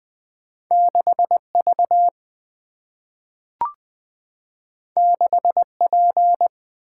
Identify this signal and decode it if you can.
Morse code, heard as a keyed tone, about 20 wpm, 700 Hz. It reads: 6 V 6 P
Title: Single-Letter Single-Number 20wpm